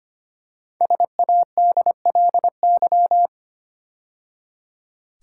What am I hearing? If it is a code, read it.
Morse code, heard as a keyed tone, about 25 wpm, 700 Hz. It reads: SADLY
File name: 5 Letter Words - First 3 Letters Given - 25wpm